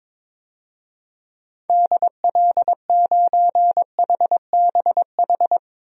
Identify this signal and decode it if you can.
Morse code, heard as a keyed tone, about 21 wpm, 700 Hz. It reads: DL9HBH
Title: DXCC-100 Call Signs - 6 Characters - 22wpm